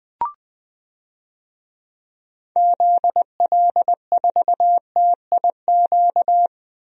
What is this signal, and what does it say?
Morse code, heard as a keyed tone, about 21 wpm, 700 Hz. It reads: ZL4TIQ
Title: DXCC-100 Call Signs - 6 Characters - Repeated 3x - 20wpm